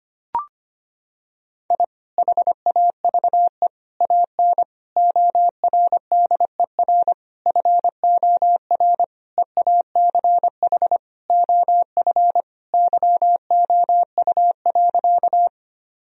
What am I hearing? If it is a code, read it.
Morse code, heard as a keyed tone, about 25 wpm, 700 Hz. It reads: I HAVE AN ORDER FOR EACH OF YOU.
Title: Sentences from Top 1000 Words - Review Part 2 - 25wpm